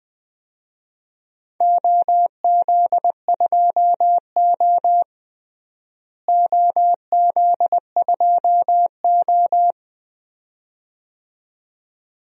Morse code, heard as a keyed tone, about 20 wpm, 700 Hz. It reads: OZ2O OZ2O
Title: DXCC-100 Call Signs - 4 Characters - Repeated 2x - 20wpm